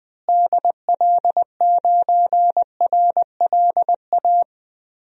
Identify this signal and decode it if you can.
Morse code, heard as a keyed tone, about 20 wpm, 700 Hz. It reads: DL9RLA
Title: DXCC-100 Call Signs - 6 Characters - Repeated 3x - 20wpm